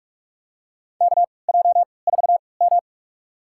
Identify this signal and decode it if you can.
Morse code, heard as a keyed tone, about 48 wpm, spaced as for 26 wpm, 690 Hz. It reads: KJ4M
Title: USA Call Signs - 4 Characters - 3x Character Spacing - 45wpm